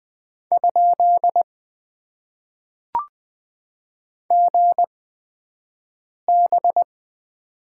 Morse code, heard as a keyed tone, about 20 wpm, spaced as for 5 wpm, 700 Hz. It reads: ? GB
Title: Top 90 QSO Elements - 8x Character Spacing - 20wpm